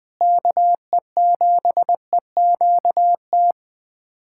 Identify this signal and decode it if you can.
Morse code, heard as a keyed tone, about 20 wpm, 700 Hz. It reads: KE7EQT
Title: USA Callsign DE Callsign - 6 Characters - 20wpm